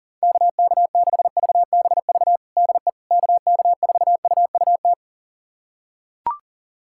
Morse code, heard as a keyed tone, about 37 wpm, 700 Hz. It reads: KK6VBV DE KK4UUT
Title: USA Callsign DE Callsign - 6 Characters - 40wpm